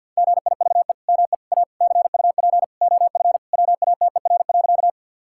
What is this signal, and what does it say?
Morse code, heard as a keyed tone, about 50 wpm, 700 Hz. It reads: GIVE ME A CUP OF WATER.
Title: Sentences from Top 1600 Words 50wpm